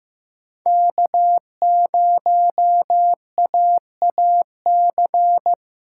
Morse code, heard as a keyed tone, about 16 wpm, 700 Hz. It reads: K0AAC
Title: USA Callsign DE Callsign - 5 Characters - 15wpm